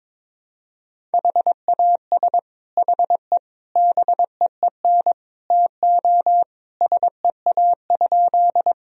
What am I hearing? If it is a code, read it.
Morse code, heard as a keyed tone, about 22 wpm, 700 Hz. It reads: HAS HE BEEN TO SEA?
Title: Sentences from Top 1000 Words - Review Part 1 - 22wpm